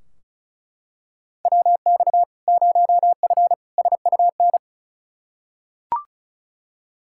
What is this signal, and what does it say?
Morse code, heard as a keyed tone, about 36 wpm, 700 Hz. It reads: WX 0F SUN
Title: Practice QSO - Exchange 2 - Info - 35wpm